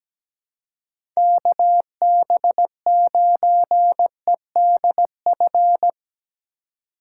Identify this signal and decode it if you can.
Morse code, heard as a keyed tone, about 17 wpm, 700 Hz. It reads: KB9EDF
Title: USA Call Signs - 6 Characters - 2x Character Spacing - 17wpm